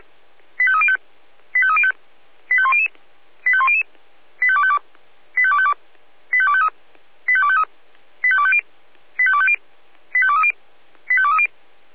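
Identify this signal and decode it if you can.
Five-tone selective call (ZVEI), heard as quick runs of five tones, tones of about 70 ms. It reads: 85305 85305 85100 85100 85252 85252 85253 85253 85250 85250 85230 85230
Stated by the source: Five-tone ZVEI selective call received by SDR.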